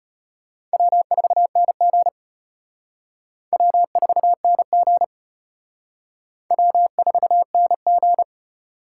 Morse code, heard as a keyed tone, about 32 wpm, 700 Hz. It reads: W4NG W4NG W4NG